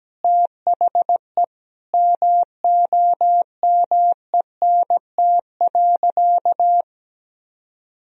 Morse code, heard as a keyed tone, about 17 wpm, 700 Hz. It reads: THE MOMENT.
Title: Sentences from Top 1000 Words - Review Part 3 - 17wpm